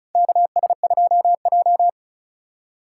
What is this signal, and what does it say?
Morse code, heard as a keyed tone, about 36 wpm, 700 Hz. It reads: KS2J
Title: USA Call Signs - 4 Characters - 2x Character Spacing - 35wpm